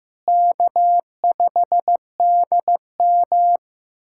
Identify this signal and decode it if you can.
Morse code, heard as a keyed tone, about 15 wpm, 700 Hz. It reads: K5DM
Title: USA Callsign DE Callsign - 4 Characters - 15wpm